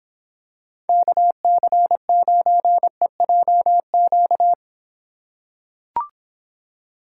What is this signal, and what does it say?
Morse code, heard as a keyed tone, about 26 wpm, 700 Hz. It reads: KC9EJQ